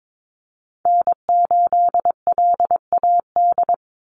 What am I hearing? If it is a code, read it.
Morse code, heard as a keyed tone, about 22 wpm, 700 Hz. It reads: N8LAD
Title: USA Call Signs 22wpm